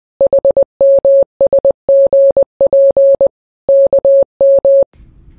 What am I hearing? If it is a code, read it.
Morse code, heard as a keyed tone, about 20 wpm, 550 Hz. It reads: HMSGPKM